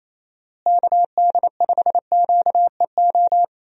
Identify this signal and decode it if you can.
Morse code, heard as a keyed tone, about 28 wpm, 700 Hz. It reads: KD5QEO
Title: USA Call Signs - 6 Characters - Repeated 3x - 28wpm